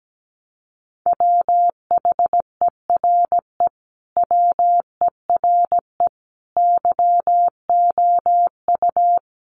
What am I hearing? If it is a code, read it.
Morse code, heard as a keyed tone, about 17 wpm, 700 Hz. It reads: WHERE WERE YOU
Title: Sentences from Top 1000 Words 17wpm